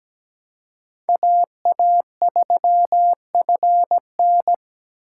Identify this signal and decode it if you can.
Morse code, heard as a keyed tone, about 17 wpm, 700 Hz. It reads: AA3FN
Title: USA Call Signs - 4x Character Spacing - 17wpm